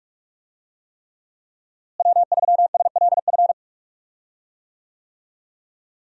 Morse code, heard as a keyed tone, about 46 wpm, 700 Hz. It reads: W3SLF
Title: USA Call Signs - 5 Characters - Repeated 3x - 45wpm